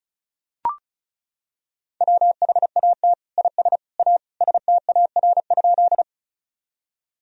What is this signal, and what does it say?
Morse code, heard as a keyed tone, about 36 wpm, 700 Hz. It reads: WHAT IS A STAR?
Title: Sentences from Top 1000 Words - Review Part 2 - 35wpm